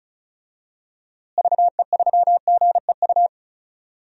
Morse code, heard as a keyed tone, about 36 wpm, 700 Hz. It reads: VE3GEU
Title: DXCC-100 Call Signs - 6 Characters - Repeated 3x - 35wpm